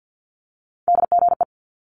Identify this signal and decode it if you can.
Morse code, heard as a keyed tone, about 50 wpm, 700 Hz. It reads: DGE
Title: Multi-letter Phonemes Quick Fire 50wpm